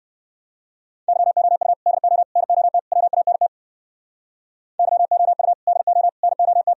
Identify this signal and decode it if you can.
Morse code, heard as a keyed tone, about 58 wpm, 700 Hz. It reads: YOU DO NOT WANT YOU DO NOT